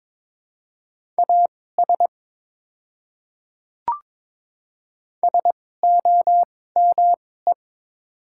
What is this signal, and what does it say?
Morse code, heard as a keyed tone, about 22 wpm, spaced as for 16 wpm, 700 Hz. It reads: AS SOME
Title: 2x Character Spacing - Top 200 Words - 22wpm